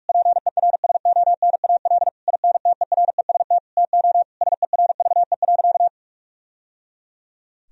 Morse code, heard as a keyed tone, about 46 wpm, 700 Hz. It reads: PERSONAL INTEREST TO SERVE.
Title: Sentences from Top 1000 Words - Review Part 4 - 45wpm